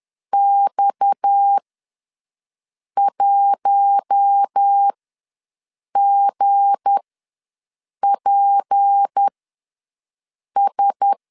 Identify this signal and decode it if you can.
Morse code, heard as a keyed tone, about 10 wpm, spaced as for 5 wpm, 800 Hz. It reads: X1GPS
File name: LEZIONI / 7 GF9Z / 7 G ws 10wpm 3843